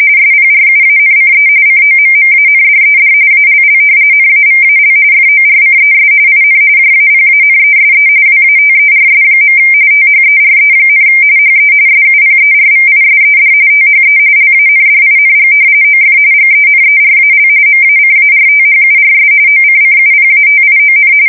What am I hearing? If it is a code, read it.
RTTY, about 45.45 baud. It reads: SPRUANCE COMMANDED ALL NAVAL FORCES IN JAPANESE WATERS. THERE, HE SUPERVISED THE LANDING OF GENERAL MACARTHURS OCCUPATION
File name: fsk170 / battleshipnj